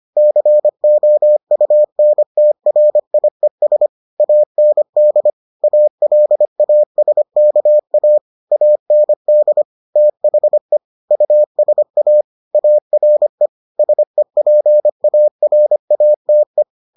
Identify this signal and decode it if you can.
Morse code, heard as a keyed tone, about 25 wpm, 600 Hz. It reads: COUNTRIES AND ALASKA AND THE USA ARE SEPARATE